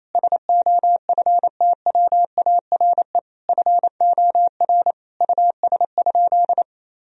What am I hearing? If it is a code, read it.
Morse code, heard as a keyed tone, about 28 wpm, 700 Hz. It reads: SOFTWARE FOR US?
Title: Sentences from Top 1000 Words - Review Part 3 - 28wpm